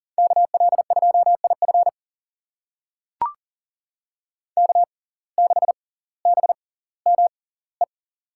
Morse code, heard as a keyed tone, about 40 wpm, spaced as for 20 wpm, 700 Hz. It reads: KL2IF K6BME
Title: USA Call Signs - 5 Characters - 6x Character Spacing - 40wpm